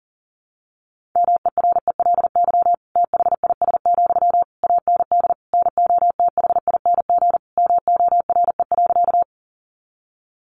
Morse code, heard as a keyed tone, about 40 wpm, 700 Hz. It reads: MERELY THIS, AND NOTHING MORE.
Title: Sentences from Top 1400 Words 40wpm